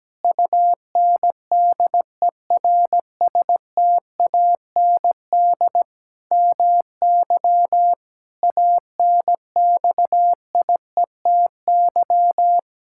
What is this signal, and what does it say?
Morse code, heard as a keyed tone, about 17 wpm, 700 Hz. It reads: UNDERSTAND MY ANXIETY
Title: Sentences from Top 1600 Words 17wpm